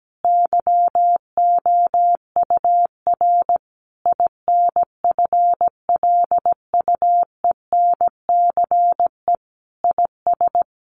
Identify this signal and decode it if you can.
Morse code, heard as a keyed tone, about 17 wpm, 700 Hz. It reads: YOUR INFLUENCE IS
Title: Sentences from Top 1100 Words 17wpm